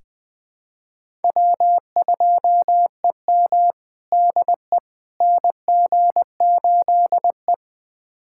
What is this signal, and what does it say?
Morse code, heard as a keyed tone, about 20 wpm, 700 Hz. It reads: W2EM DE NG8E
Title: USA Callsign DE Callsign - 4 Characters - 20wpm